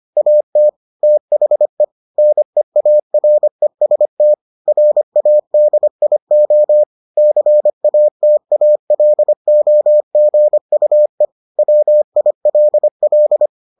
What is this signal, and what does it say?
Morse code, heard as a keyed tone, about 25 wpm, 600 Hz. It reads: AT THE NEAREST RADIO CATALOGUE WILL